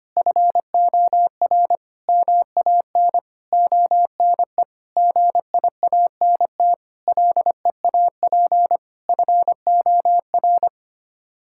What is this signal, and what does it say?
Morse code, heard as a keyed tone, about 25 wpm, 700 Hz. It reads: FOR MAN ONE GIANT LEAP FOR
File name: Sets of 9 Words - 25wpm